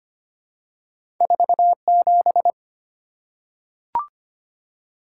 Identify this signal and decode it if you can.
Morse code, heard as a keyed tone, about 25 wpm, 700 Hz. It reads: 47
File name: Two Letter-Number Combinations - 25wpm